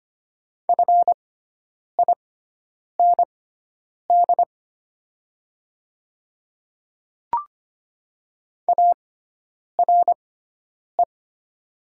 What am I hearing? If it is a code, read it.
Morse code, heard as a keyed tone, about 25 wpm, spaced as for 9 wpm, 700 Hz. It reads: FIND ARE
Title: 6x Character Spacing - Top 100 Words - 25wpm